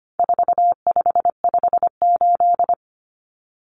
Morse code, heard as a keyed tone, about 25 wpm, 700 Hz. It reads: 4558
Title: Numbers - 4-Digit 25wpm